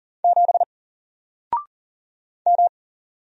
Morse code, heard as a keyed tone, about 40 wpm, 700 Hz. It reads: 7 M
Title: Single Letter-Number Alt 40wpm